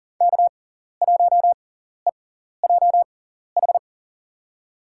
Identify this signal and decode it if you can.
Morse code, heard as a keyed tone, about 37 wpm, spaced as for 14 wpm, 700 Hz. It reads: K1EJH
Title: USA Call Signs - 5 Characters - 6x Character Spacing - 40wpm